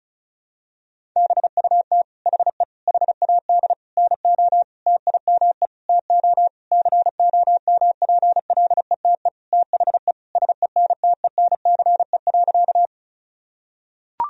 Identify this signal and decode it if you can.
Morse code, heard as a keyed tone, about 34 wpm, 700 Hz. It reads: BUT HE HAD NO TIME TO COMPLETE THE SENTENCE.
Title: Sentences from Top 1000 Words - Review Part 1 - 35wpm